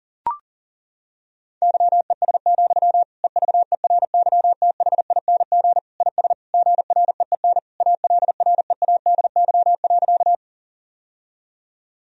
Morse code, heard as a keyed tone, about 40 wpm, 700 Hz. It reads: YES, EVERYTHING IS GREEN ALREADY.
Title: Sentences from Top 1000 Words - Review Part 2 - 40wpm